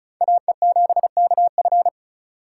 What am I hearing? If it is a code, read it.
Morse code, heard as a keyed tone, about 37 wpm, 700 Hz. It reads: AE7KF
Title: Speed Racing - USA 2x2 Call Signs 30wpm